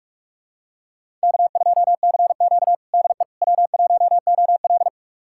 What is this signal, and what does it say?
Morse code, heard as a keyed tone, about 46 wpm, 700 Hz. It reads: K2CQ DE W1OL
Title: USA Callsign DE Callsign - 4 Characters - 45wpm